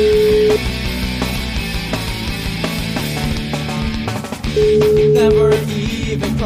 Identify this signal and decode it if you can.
Ringback tone (Europe, 425 Hz), with noise.